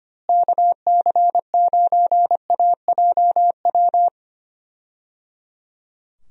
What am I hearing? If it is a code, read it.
Morse code, heard as a keyed tone, about 25 wpm, 700 Hz. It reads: KC9AJW
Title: USA Call Signs - 6 Characters - Repeated 3x - 25wpm